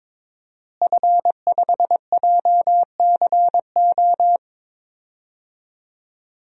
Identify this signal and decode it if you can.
Morse code, heard as a keyed tone, about 22 wpm, 700 Hz. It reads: F5JCO
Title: DXCC-100 Call Signs - 5 Characters - Repeated 3x - 22wpm